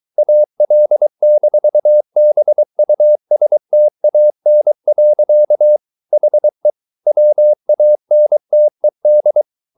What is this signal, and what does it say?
Morse code, heard as a keyed tone, about 23 wpm, 600 Hz. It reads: AL-BUSTAN. HE WANTED